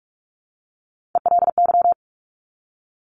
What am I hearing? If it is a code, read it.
Morse code, heard as a keyed tone, about 46 wpm, 690 Hz. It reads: ELY
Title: Top Three Letter Combinations in Words 45wpm